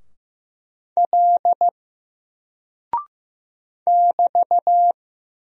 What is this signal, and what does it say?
Morse code, heard as a keyed tone, about 15 wpm, 700 Hz. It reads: L <BT>
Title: Single Character (Letter:Number:Punctuation:Prosign) 15wpm